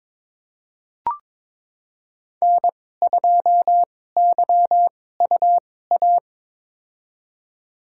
Morse code, heard as a keyed tone, about 22 wpm, spaced as for 16 wpm, 700 Hz. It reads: N2YUA
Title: USA Call Signs - 2x Character Spacing - 22wpm